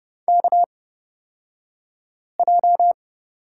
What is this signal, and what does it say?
Morse code, heard as a keyed tone, about 30 wpm, spaced as for 5 wpm, 700 Hz. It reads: KJ
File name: Lesson 36d - BT - Two Characters 30wpm